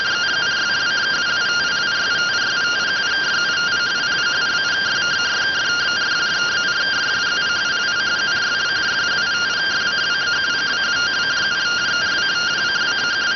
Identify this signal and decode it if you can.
Two-tone teleprinter signal, about 170 Hz shift.